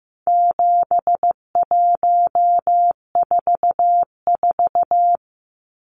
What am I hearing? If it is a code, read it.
Morse code, heard as a keyed tone, about 15 wpm, 700 Hz. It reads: 7144
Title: Numbers - 4-Digit 15wpm